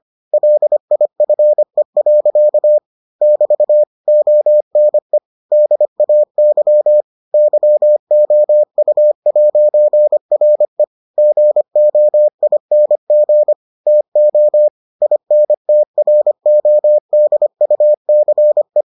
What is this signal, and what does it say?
Morse code, heard as a keyed tone, about 25 wpm, 600 Hz. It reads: LIFE. <BT> ONE DAY YOU'RE GOING TO INTRODUCE